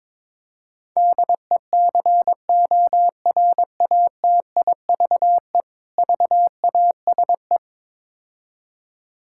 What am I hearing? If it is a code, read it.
Morse code, heard as a keyed tone, about 21 wpm, 700 Hz. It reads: DECORATIVE VASE
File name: Difficult Words with Hints - 22wpm